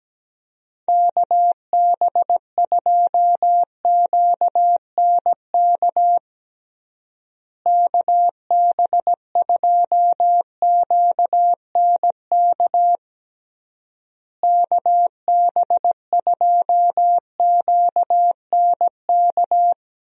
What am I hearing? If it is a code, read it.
Morse code, heard as a keyed tone, about 17 wpm, 700 Hz. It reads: KB2QNK KB2QNK KB2QNK
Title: USA Call Signs - 6 Characters - Repeated 3x - 17wpm